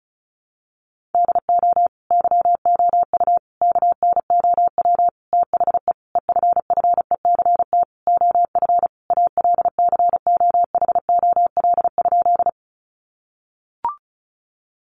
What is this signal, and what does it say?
Morse code, heard as a keyed tone, about 34 wpm, 700 Hz. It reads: DO YOU KNOW THE EFFECT OF ALCOHOL?
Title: Sentences from Top 1400 Words 35wpm